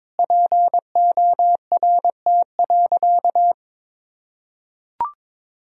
Morse code, heard as a keyed tone, about 22 wpm, 700 Hz. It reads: PORT.
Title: Sentences from Top 1000 Words - Review Part 2 - 22wpm